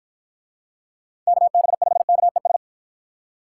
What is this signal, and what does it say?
Morse code, heard as a keyed tone, about 53 wpm, 700 Hz. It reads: KB5KES